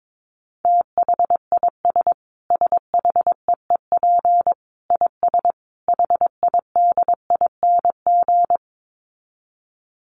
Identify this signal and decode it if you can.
Morse code, heard as a keyed tone, about 22 wpm, 700 Hz. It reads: THIS SHEEP IS HIDING
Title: Dit Challenge 22wpm